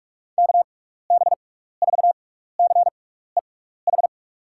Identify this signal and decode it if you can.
Morse code, heard as a keyed tone, about 42 wpm, spaced as for 15 wpm, 700 Hz. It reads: KB4CEH
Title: USA Call Signs - 6 Characters - 6x Character Spacing - 45wpm